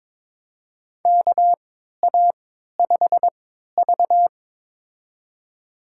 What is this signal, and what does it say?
Morse code, heard as a keyed tone, about 22 wpm, spaced as for 13 wpm, 700 Hz. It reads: KA5V
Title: USA Call Signs - 3x Character Spacing - 22wpm